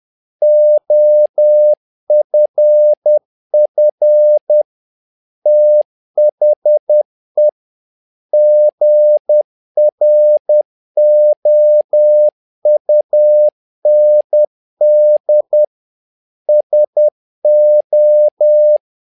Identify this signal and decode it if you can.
Morse code, heard as a keyed tone, about 10 wpm, 600 Hz. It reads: OFF THE GROUND SO
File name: learning center / tower trouble / tower trouble 10wpm